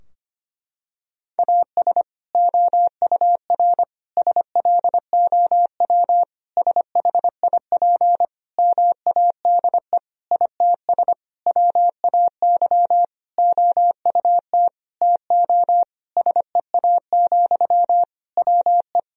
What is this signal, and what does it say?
Morse code, heard as a keyed tone, about 25 wpm, 700 Hz. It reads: AS OUR SLOW SHIP MADE ITS WAY OUT TO SEA, WE
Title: Sentences from Top 1000 Words - Review Part 2 - 25wpm